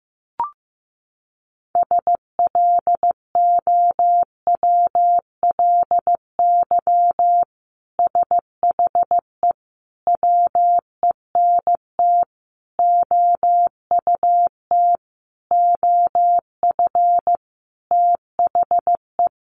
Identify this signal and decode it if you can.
Morse code, heard as a keyed tone, about 15 wpm, 700 Hz. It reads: SLOWLY SHE WENT OUT OF THE
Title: Sentences from Top 1100 Words 15wpm